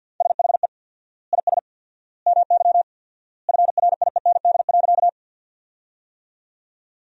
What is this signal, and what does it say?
Morse code, heard as a keyed tone, about 48 wpm, 700 Hz. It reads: SHE IS MY FRIEND.
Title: 4x Word-Spacing - Sentences from Top 300 Words - 50wpm